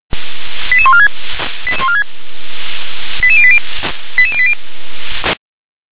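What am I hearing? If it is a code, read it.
Five-tone selective call (ZVEI), heard as quick runs of five tones, tones of about 70 ms. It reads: 88146 88080 88080
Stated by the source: Five-tone ZVEI call 88152, received by SDR.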